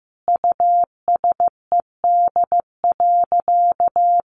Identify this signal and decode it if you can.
Morse code, heard as a keyed tone, about 15 wpm, 700 Hz. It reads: USED.
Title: Sentences from Top 600 Words 15wpm